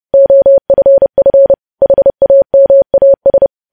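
Morse code, heard as a keyed tone, about 30 wpm, 550 Hz. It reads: OFF HAMAS
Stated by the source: Morse code news headlines practice sent at 30 wpm.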